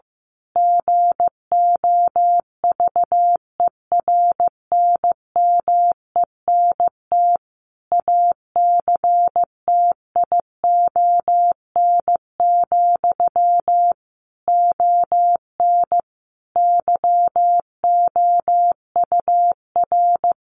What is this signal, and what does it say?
Morse code, heard as a keyed tone, about 15 wpm, 700 Hz. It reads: GOVERNMENT ACTION, ON YOUR
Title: A Sound Of Thunder 15wpm